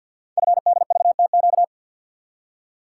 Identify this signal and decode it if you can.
Morse code, heard as a keyed tone, about 48 wpm, 700 Hz. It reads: FDVTQ
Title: Code Groups - 5 Letters - 50wpm